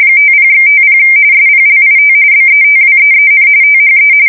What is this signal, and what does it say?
RTTY, about 45.45 baud. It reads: 50/50 SPLIT BETWEEN THE
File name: fsk170 / usnavy5